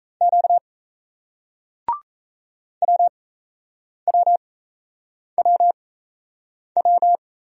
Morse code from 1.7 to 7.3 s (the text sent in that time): WWWW